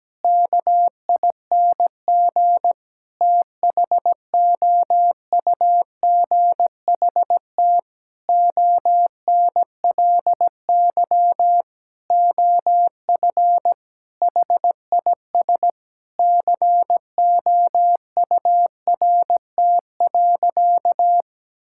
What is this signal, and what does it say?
Morse code, heard as a keyed tone, about 17 wpm, 700 Hz. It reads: KING THOUGHT ONLY OF HIS COURT.
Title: Sentences from Top 1600 Words 17wpm